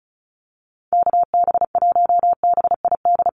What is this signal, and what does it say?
Morse code, heard as a keyed tone, about 34 wpm, 700 Hz. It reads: KB1BID
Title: USA Call Signs Repeated 2x 35wpm